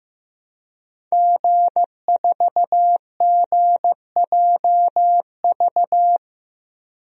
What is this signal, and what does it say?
Morse code, heard as a keyed tone, about 15 wpm, 700 Hz. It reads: G4GJV
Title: DXCC-100 Call Signs - 5 Characters - Repeated 3x - 15wpm